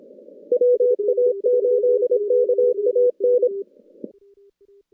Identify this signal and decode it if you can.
Morse code, heard as a keyed tone, about 26 wpm, 490 Hz. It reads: WA8KAN
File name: contests / CW-SS-k5tr- -2100